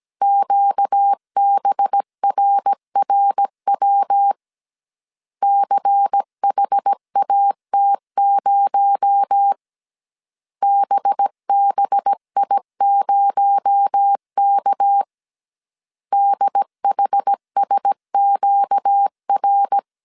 Morse code, heard as a keyed tone, about 17 wpm, 800 Hz. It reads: QBRRW CHAT0 BBI0K DHSQR
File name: 16wpm